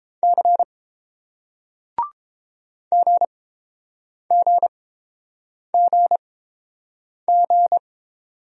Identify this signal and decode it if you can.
Morse code, heard as a keyed tone, about 31 wpm, spaced as for 7 wpm, 700 Hz. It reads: C GGGG